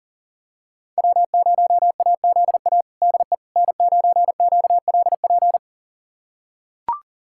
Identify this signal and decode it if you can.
Morse code, heard as a keyed tone, about 40 wpm, 700 Hz. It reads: W0AZA DE N9QLP
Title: USA Callsign DE Callsign - 5 Characters - 40wpm